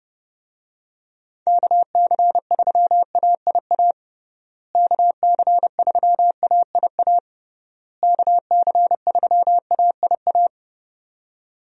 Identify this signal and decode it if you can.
Morse code, heard as a keyed tone, about 32 wpm, 700 Hz. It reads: KC3AIA KC3AIA KC3AIA
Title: USA Call Signs - 6 Characters - Repeated 3x - 30wpm